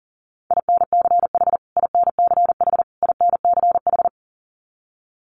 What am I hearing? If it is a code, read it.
Morse code, heard as a keyed tone, about 37 wpm, 700 Hz. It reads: INCH INCH INCH
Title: Top 500 Words Repeated 40wpm